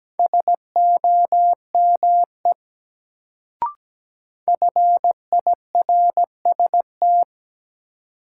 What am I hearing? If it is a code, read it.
Morse code, heard as a keyed tone, about 17 wpm, 700 Hz. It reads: SOME FIRST
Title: Sentences from Top 100 Words - New Word Drill - 17wpm